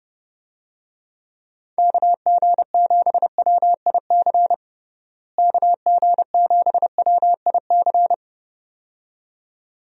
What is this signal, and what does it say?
Morse code, heard as a keyed tone, about 30 wpm, 700 Hz. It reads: KG7WIC KG7WIC
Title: USA Call Signs - 6 Characters - Repeated 2x - 30wpm